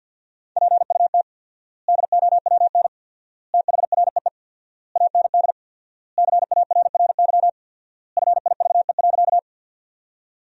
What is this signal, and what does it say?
Morse code, heard as a keyed tone, about 50 wpm, 700 Hz. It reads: PUT DOWN THREE AND CARRY FIVE.
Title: 4x Word-Spacing - Sentences from Top 400 Words - 50wpm